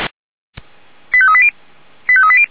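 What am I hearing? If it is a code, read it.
Five-tone selective call (ZVEI), heard as quick runs of five tones, tones of about 70 ms. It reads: 85380 85380
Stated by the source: ZVEI five-tone call 89494 received by SDR, channel 464, recorded by feuerwehr-orlamuende fire brigade server.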